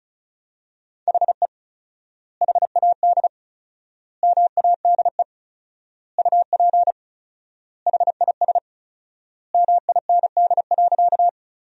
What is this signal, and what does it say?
Morse code, heard as a keyed tone, about 36 wpm, 700 Hz. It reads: HE HAD MADE UP HIS MIND.
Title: 4x Word-Spacing - Sentences from Top 500 Words - 35wpm